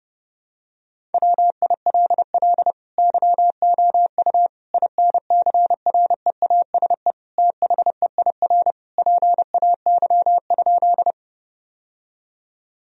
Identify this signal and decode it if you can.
Morse code, heard as a keyed tone, about 30 wpm, 700 Hz. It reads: WILL YOU INCREASE THEIR PAY?
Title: Sentences from Top 1000 Words - Review Part 4 - 30wpm